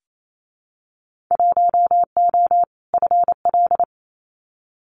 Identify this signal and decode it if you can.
Morse code, heard as a keyed tone, about 28 wpm, 700 Hz. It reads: 1O FL
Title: Winter Field Day Exchanges 28wpm